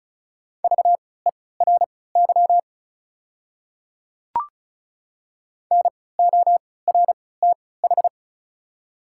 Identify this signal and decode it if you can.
Morse code, heard as a keyed tone, about 36 wpm, spaced as for 20 wpm, 700 Hz. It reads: VERY NORTH
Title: 3x Character Spacing - Top 300 Words - 35wpm